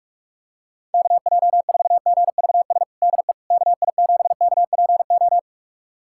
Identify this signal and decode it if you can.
Morse code, heard as a keyed tone, about 41 wpm, 700 Hz. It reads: KJ4GVS DE KI7KPO
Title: USA Callsign DE Callsign - 6 Characters - 45wpm